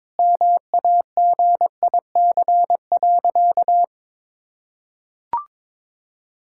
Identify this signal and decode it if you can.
Morse code, heard as a keyed tone, about 22 wpm, 700 Hz. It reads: MAGIC.
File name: practice / 22wpm / Sentences with 4-Word Phrases - 22wpm